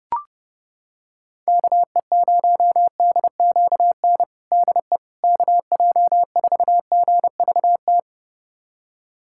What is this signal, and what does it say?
Morse code, heard as a keyed tone, about 30 wpm, 700 Hz. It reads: KE0DQN DE KJ4GVT
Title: USA Callsign DE Callsign - 6 Characters - 30wpm